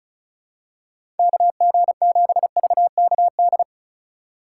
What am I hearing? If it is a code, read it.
Morse code, heard as a keyed tone, about 36 wpm, 700 Hz. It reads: KG7VKD
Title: USA Call Signs - 6 Characters - 4x Character Spacing - 35wpm